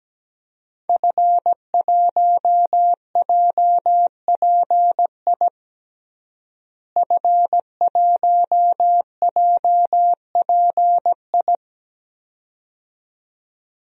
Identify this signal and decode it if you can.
Morse code, heard as a keyed tone, about 17 wpm, 700 Hz. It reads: F1JPI F1JPI
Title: DXCC-100 Call Signs - 5 Characters - Repeated 2x - 17wpm